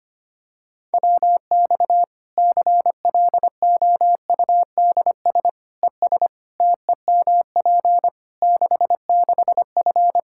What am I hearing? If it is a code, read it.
Morse code, heard as a keyed tone, about 25 wpm, 700 Hz. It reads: WX CLOUDS ES TEMP 66F
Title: Practice QSO - Exchange 2 - Info - 25wpm